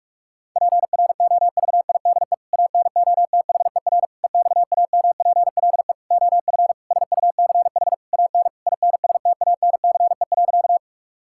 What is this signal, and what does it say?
Morse code, heard as a keyed tone, about 41 wpm, 700 Hz. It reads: PROVIDE ANOTHER EXAMPLE OF SUCH AN INSTANCE.
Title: Sentences from Top 1000 Words - Review Part 5 - 45wpm